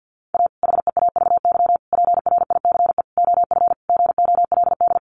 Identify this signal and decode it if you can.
Morse code, heard as a keyed tone, about 50 wpm, 700 Hz. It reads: A HEAVY PRICE OF GOLD
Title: Sentences from Top 1100 Words 50wpm